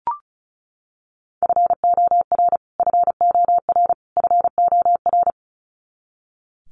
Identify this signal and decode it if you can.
Morse code, heard as a keyed tone, about 34 wpm, 700 Hz. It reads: FOR FOR FOR
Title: Top 200 Words Repeated 35wpm